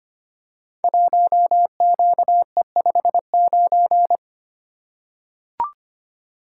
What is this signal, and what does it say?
Morse code, heard as a keyed tone, about 25 wpm, 700 Hz. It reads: 1QE59
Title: Code Groups - 5 Letter:Numbers - 25wpm